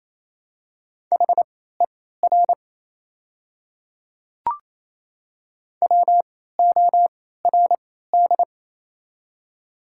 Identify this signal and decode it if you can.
Morse code, heard as a keyed tone, about 28 wpm, spaced as for 16 wpm, 700 Hz. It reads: HER WORD
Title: 3x Character Spacing - Top 100 Words - 28wpm